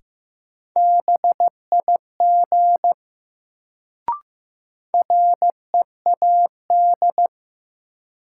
Morse code, heard as a keyed tone, about 15 wpm, 700 Hz. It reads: BIG READ
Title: Sentences from Top 200 Words - New Word Drill - 15wpm